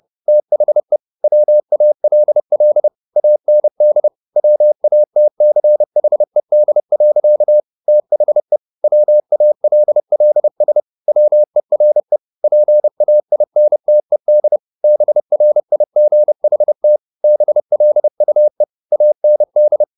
Morse code, heard as a keyed tone, about 30 wpm, 600 Hz. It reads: THE WALL AND WATCHED. THE WALLS WERE PAINTED BRIGHT BLUE AND